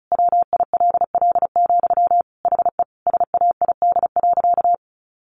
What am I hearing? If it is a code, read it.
Morse code, heard as a keyed tone, about 34 wpm, 700 Hz. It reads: WILL, HE SAID.
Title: Sentences from Top 200 Words 35wpm